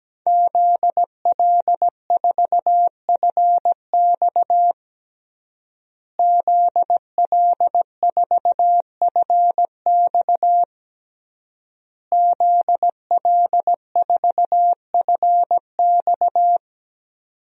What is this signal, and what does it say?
Morse code, heard as a keyed tone, about 17 wpm, 700 Hz. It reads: ZL4FX ZL4FX ZL4FX
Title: DXCC-100 Call Signs - 5 Characters - Repeated 3x - 17wpm